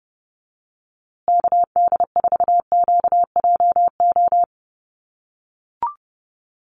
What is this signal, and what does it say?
Morse code, heard as a keyed tone, about 29 wpm, 700 Hz. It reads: KD4QJO
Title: USA Call Signs Repeated 2x 30wpm